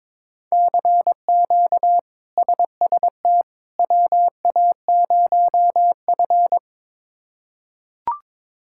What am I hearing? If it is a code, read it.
Morse code, heard as a keyed tone, about 22 wpm, 700 Hz. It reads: CQ SST WA0F
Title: K1USN SST QSO Practice - 22wpm